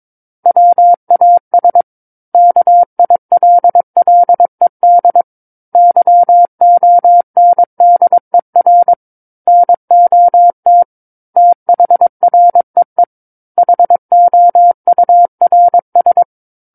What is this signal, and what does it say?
Morse code, heard as a keyed tone, about 22 wpm, 700 Hz. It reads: WAS KILLED YONDER NOT THREE HOURS
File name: Book One / 13-Chapter 13 - 22 WPM